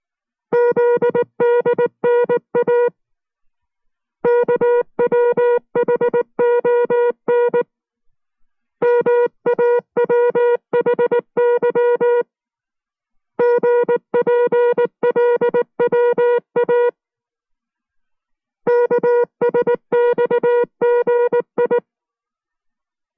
Morse code, heard as a keyed tone, about 19 wpm, 480 Hz. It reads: ZDNA KWHON MAWHY GPLWA KSXGI